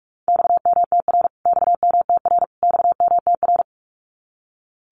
Morse code, heard as a keyed tone, about 44 wpm, 700 Hz. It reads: XMTR XMTR XMTR
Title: Abbreviations Repeated 3x - 45wpm